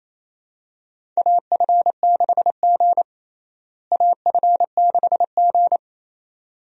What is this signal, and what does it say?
Morse code, heard as a keyed tone, about 28 wpm, 700 Hz. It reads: AF6G AF6G
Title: USA Call Signs - 4 Characters - Repeated 2x - 28wpm